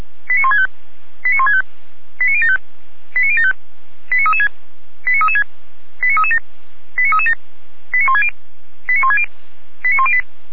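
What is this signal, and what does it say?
Five-tone selective call (ZVEI), heard as quick runs of five tones, tones of about 70 ms. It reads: 89165 89165 89075 89075 89336 89336 89337 89337 89160 89160 89108
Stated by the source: ZVEI five-tone selective call 89150 received by SDR.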